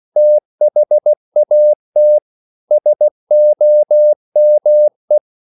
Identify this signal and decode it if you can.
Morse code, heard as a keyed tone, about 16 wpm, 600 Hz. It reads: THAT SOME
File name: CW / Practice / 500MostCommonEnglishWords16